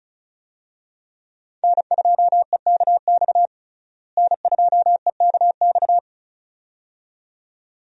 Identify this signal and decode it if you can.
Morse code, heard as a keyed tone, about 36 wpm, 700 Hz. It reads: N2EKX N2EKX
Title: USA Call Signs - 5 Characters - Repeated 2x - 35wpm